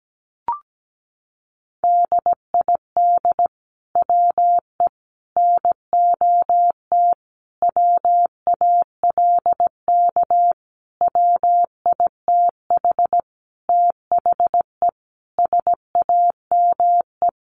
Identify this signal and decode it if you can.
Morse code, heard as a keyed tone, about 17 wpm, 700 Hz. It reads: DID WE NOT WALK WITH THE SAME
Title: Sentences from Top 1400 Words 17wpm